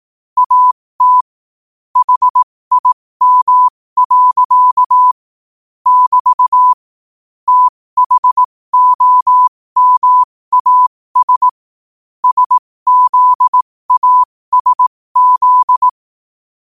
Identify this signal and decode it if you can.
Morse code, heard as a keyed tone, about 18 wpm, spaced as for 15 wpm, 1000 Hz. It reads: AT HIM. <BT> THOMAS SZASZ
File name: qotdfiles / 15wpm